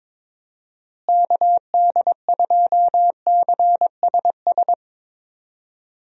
Morse code, heard as a keyed tone, about 22 wpm, 700 Hz. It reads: KD2CSS
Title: USA Call Signs - 6 Characters - Repeated 2x - 22wpm